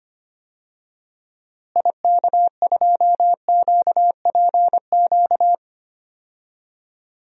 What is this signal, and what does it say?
Morse code, heard as a keyed tone, about 25 wpm, 700 Hz. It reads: IK2QPQ
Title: DXCC-100 Call Signs - 6 Characters - Repeated 2x - 25wpm